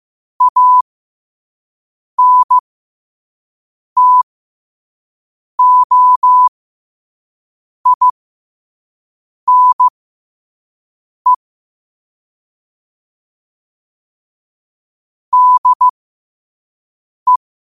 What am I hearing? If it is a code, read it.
Morse code, heard as a keyed tone, about 15 wpm, spaced as for 5 wpm, 1000 Hz. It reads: ANTOINE DE
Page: New quotes every day in morse code at 5 Words per minute.